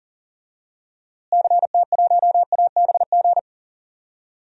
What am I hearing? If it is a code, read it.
Morse code, heard as a keyed tone, about 40 wpm, 700 Hz. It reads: CT1ABG
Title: DXCC-100 Call Signs - 6 Characters - Repeated 2x - 40wpm